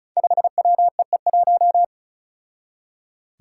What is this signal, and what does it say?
Morse code, heard as a keyed tone, about 36 wpm, 700 Hz. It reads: 5WEE1
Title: Five Challenging Letter-Number Combinations 35wpm